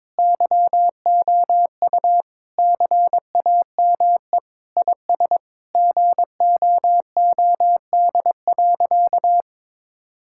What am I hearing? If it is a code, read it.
Morse code, heard as a keyed tone, about 22 wpm, 700 Hz. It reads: YOU CAME IS GOOD.
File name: practice / 22wpm / Sentences with 3-Word Phrases - 22wpm